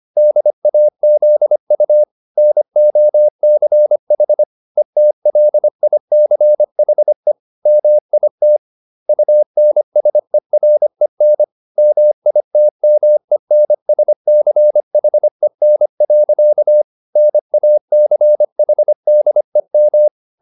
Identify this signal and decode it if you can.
Morse code, heard as a keyed tone, about 25 wpm, 600 Hz. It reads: DAZU NOCH ETLICHE MIT UNSEREN MITMENSCHEN. NACHDEM